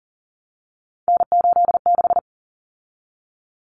Morse code, heard as a keyed tone, about 40 wpm, 700 Hz. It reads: N86
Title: Common Cut Numbers - 3-Digit 40wpm